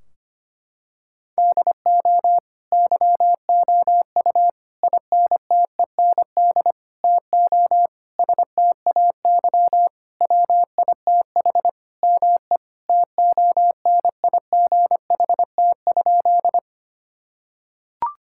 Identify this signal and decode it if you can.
Morse code, heard as a keyed tone, about 25 wpm, 700 Hz. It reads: DO YOU INTEND TO STAY WITH ME TONIGHT?
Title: Sentences from Top 1600 Words 25wpm